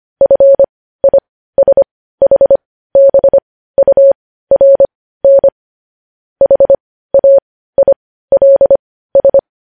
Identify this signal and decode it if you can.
Morse code, heard as a keyed tone, about 25 wpm, spaced as for 15 wpm, 550 Hz. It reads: FISHBURN HAILS
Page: news headlines in morse code at 15 wpm